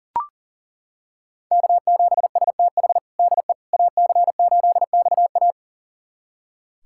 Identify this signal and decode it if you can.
Morse code, heard as a keyed tone, about 40 wpm, 700 Hz. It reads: K7STH DE AC8XA
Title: USA Callsign DE Callsign - 5 Characters - 40wpm